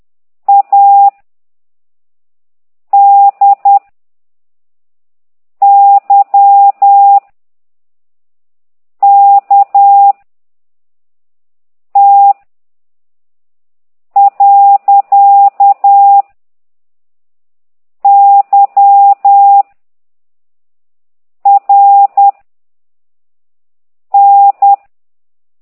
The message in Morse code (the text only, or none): A D Y K T . Y R N